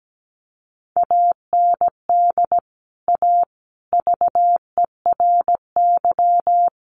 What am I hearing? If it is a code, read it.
Morse code, heard as a keyed tone, about 17 wpm, 700 Hz. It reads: AND A VERY
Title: Sentences from Top 200 Words 17wpm